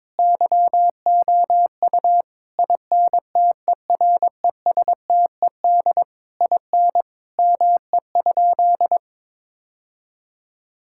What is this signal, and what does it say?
Morse code, heard as a keyed tone, about 22 wpm, 700 Hz. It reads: YOU INTERESTED IN ME?
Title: Sentences from Top 1000 Words - Review Part 6 - 22wpm